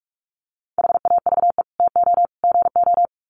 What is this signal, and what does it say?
Morse code, heard as a keyed tone, about 44 wpm, 700 Hz. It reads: HAVE TO GO
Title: Sets of 3 Words 45wpm